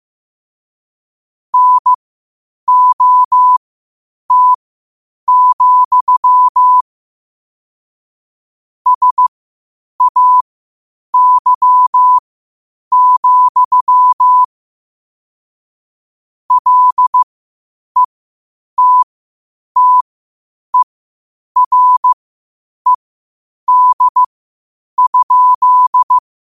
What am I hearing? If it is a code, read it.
Morse code, heard as a keyed tone, about 15 wpm, spaced as for 8 wpm, 1000 Hz. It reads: NOT, SAY, LETTERED?
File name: qotdfiles / 8wpm